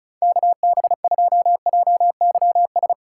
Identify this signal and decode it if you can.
Morse code, heard as a keyed tone, about 36 wpm, 690 Hz. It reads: KB2JYS
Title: USA Call Signs - 6 Characters - 6x Character Spacing - 35wpm